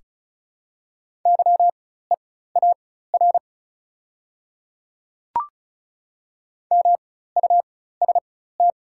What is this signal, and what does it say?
Morse code, heard as a keyed tone, about 36 wpm, spaced as for 16 wpm, 690 Hz. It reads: YEAR MUST
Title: 4x Character Spacing - Top 200 Words - 35wpm